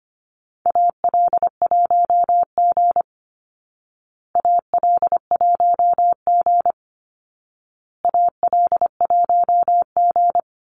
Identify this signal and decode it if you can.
Morse code, heard as a keyed tone, about 25 wpm, 700 Hz. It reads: AL1G AL1G AL1G
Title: Call Signs Repeated 2x - 25wpm